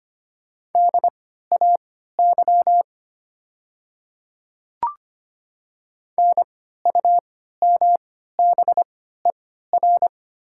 Morse code, heard as a keyed tone, about 25 wpm, spaced as for 14 wpm, 700 Hz. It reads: DAY NUMBER
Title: 3x Character Spacing - Top 100 Words - 25wpm